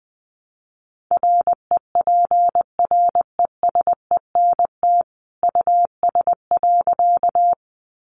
Morse code, heard as a keyed tone, about 20 wpm, 700 Hz. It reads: REPRESENT US.
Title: Sentences from Top 1200 Words 20wpm